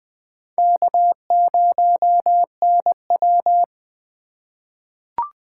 Morse code, heard as a keyed tone, about 21 wpm, 700 Hz. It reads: K0NW
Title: K1USN SST QSO Practice - 20wpm